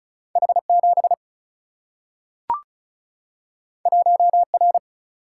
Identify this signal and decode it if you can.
Morse code, heard as a keyed tone, about 36 wpm, 700 Hz. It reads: H 7 1 R
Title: Single-Letter Single-Number 35wpm